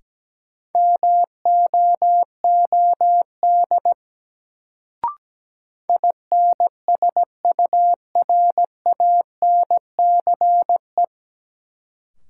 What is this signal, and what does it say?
Morse code, heard as a keyed tone, about 17 wpm, 700 Hz. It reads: MOOD INSURANCE
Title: Sentences from Top 1100 Words - New Word Drill Alt - 17wpm